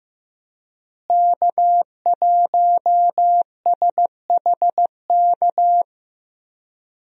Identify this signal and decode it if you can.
Morse code, heard as a keyed tone, about 15 wpm, 700 Hz. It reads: K1SHK
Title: USA Call Signs - 5 Characters - Repeated 2x - 15wpm